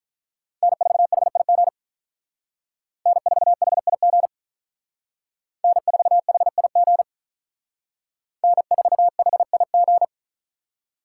Morse code, heard as a keyed tone, about 45 wpm, 700 Hz. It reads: N4HIG N4HIG N4HIG N4HIG